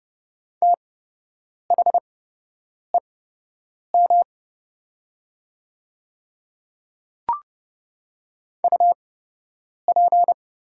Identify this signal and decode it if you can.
Morse code, heard as a keyed tone, about 29 wpm, spaced as for 8 wpm, 700 Hz. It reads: THEM UP
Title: 8x Character Spacing - Top 100 Words - 30wpm